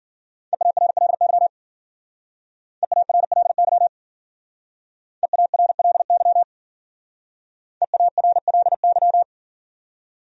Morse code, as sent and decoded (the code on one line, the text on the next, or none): . .- .-. .-.. -.-- / . .- .-. .-.. -.-- / . .- .-. .-.. -.-- / . .- .-. .-.. -.--
EARLY EARLY EARLY EARLY